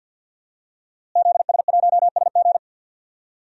Morse code, heard as a keyed tone, about 52 wpm, 700 Hz. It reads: ZS1SG
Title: DXCC-100 Call Signs - 5 Characters - Repeated 2x - 50wpm